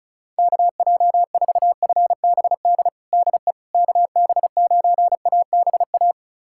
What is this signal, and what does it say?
Morse code, heard as a keyed tone, about 34 wpm, 700 Hz. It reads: KJ4FBD DE KB9ABA
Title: USA Callsign DE Callsign - 6 Characters - 35wpm